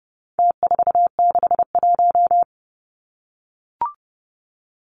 Morse code, heard as a keyed tone, about 30 wpm, 700 Hz. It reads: T461
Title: Numbers - Contest Style - 4-Digit 30wpm